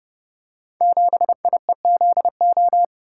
Morse code, heard as a keyed tone, about 30 wpm, 700 Hz. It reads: 7IEZO
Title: Code Groups - 5 Letter:Numbers - 30wpm